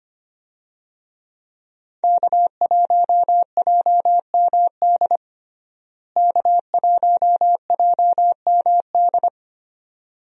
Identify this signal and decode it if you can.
Morse code, heard as a keyed tone, about 25 wpm, 700 Hz. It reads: K1JMD K1JMD
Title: USA Call Signs - 5 Characters - Repeated 2x - 25wpm